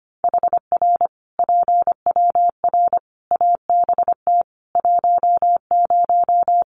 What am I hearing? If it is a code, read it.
Morse code, heard as a keyed tone, about 25 wpm, 700 Hz. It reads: HR PWR ABT 10
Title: Common QSO Phrases 25wpm